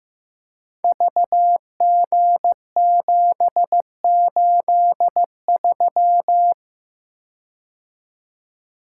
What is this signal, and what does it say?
Morse code, heard as a keyed tone, about 15 wpm, 700 Hz. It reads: VG783
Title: Code Groups - 5 Letter:Numbers - 15wpm